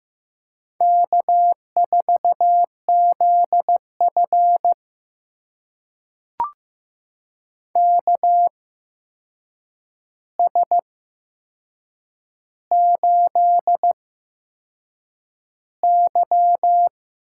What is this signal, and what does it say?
Morse code, heard as a keyed tone, about 15 wpm, spaced as for 6 wpm, 700 Hz. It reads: K4ZF KS8Y
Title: USA Call Signs - 4 Characters - 8x Character Spacing - 15wpm